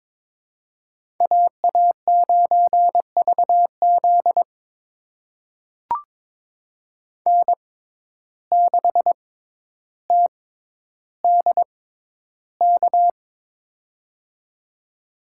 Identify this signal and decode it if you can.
Morse code, heard as a keyed tone, about 22 wpm, spaced as for 11 wpm, 700 Hz. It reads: AA9VZ N6TDK
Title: USA Call Signs - 5 Characters - 6x Character Spacing - 22wpm